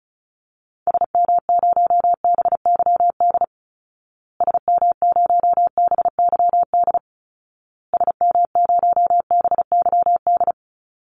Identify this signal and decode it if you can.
Morse code, heard as a keyed tone, about 34 wpm, 700 Hz. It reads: SM0BYD SM0BYD SM0BYD
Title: Call Signs Repeated 2x - 35wpm